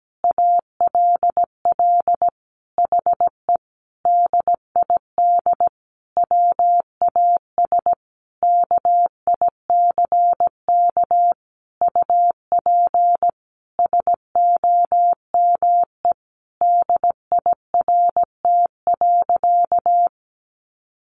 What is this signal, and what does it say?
Morse code, heard as a keyed tone, about 17 wpm, 700 Hz. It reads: ALL HE DID WAS KICK UP SOME DIRT.
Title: A Sound Of Thunder 17wpm